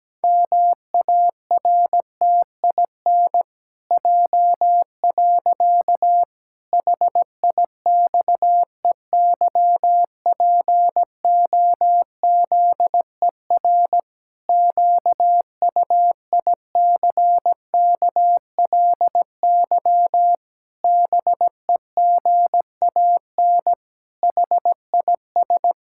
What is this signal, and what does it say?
Morse code, heard as a keyed tone, about 17 wpm, 700 Hz. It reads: MARTIN J. HIXEYPOZER QUICKLY BEGAN HIS
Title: Pangram Challenge v2 - 17wpm